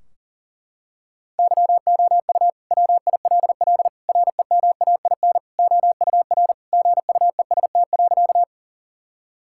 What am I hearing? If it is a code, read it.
Morse code, heard as a keyed tone, about 37 wpm, 700 Hz. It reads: YOU WILL REMAIN OUR GUEST.
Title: Sentences from Top 1600 Words 40wpm